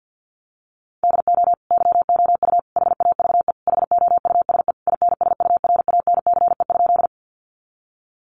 Morse code, heard as a keyed tone, about 50 wpm, 700 Hz. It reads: DO YOU HAVE HOUSE INSURANCE?
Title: Sentences from Top 1100 Words 50wpm